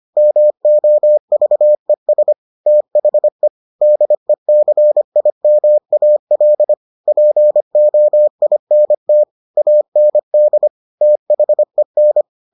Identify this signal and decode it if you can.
Morse code, heard as a keyed tone, about 25 wpm, 600 Hz. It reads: MOVES THE DECIMAL POINT AND THEN